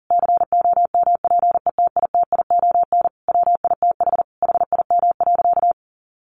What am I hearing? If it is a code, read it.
Morse code, heard as a keyed tone, about 40 wpm, 700 Hz. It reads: COMPETITION WITH HIM.
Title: Sentences from Top 1000 Words 40wpm